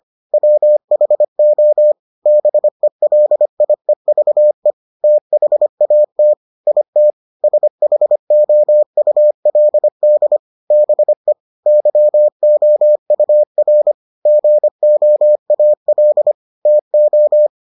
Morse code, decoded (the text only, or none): WHO BELIEVE THAT IT SHOULD BE YOUR GOAL TO